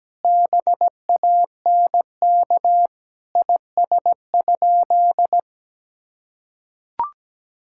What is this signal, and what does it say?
Morse code, heard as a keyed tone, about 17 wpm, 700 Hz. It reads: BANK IS?
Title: Sentences from Top 1000 Words - Review Part 6 - 17wpm